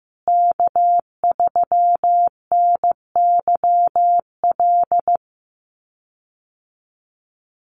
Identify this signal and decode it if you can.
Morse code, heard as a keyed tone, about 15 wpm, 700 Hz. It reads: K3NYL
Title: USA Call Signs Repeated 2x 15wpm